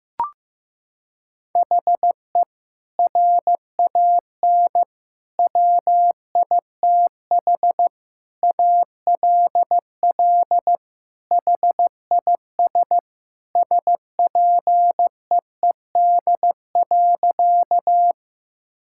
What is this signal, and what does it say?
Morse code, heard as a keyed tone, about 16 wpm, 700 Hz. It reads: HE RAN WITH ALL HIS SPEED.
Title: Sentences from Top 1000 Words - Review Part 5 - 15wpm